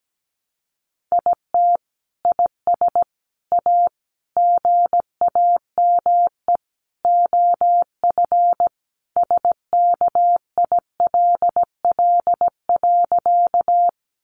Morse code, heard as a keyed tone, about 17 wpm, 700 Hz. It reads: IT IS A GAME OF SKILL.
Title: Sentences from Top 1200 Words 17wpm